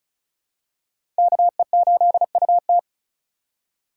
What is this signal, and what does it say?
Morse code, heard as a keyed tone, about 34 wpm, 700 Hz. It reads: KE8UT
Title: USA Call Signs - 5 Characters - 4x Character Spacing - 35wpm